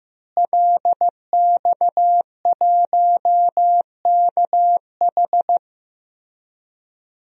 Morse code, heard as a keyed tone, about 15 wpm, 700 Hz. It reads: LX1KH
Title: DXCC-100 Call Signs - 5 Characters - Repeated 3x - 15wpm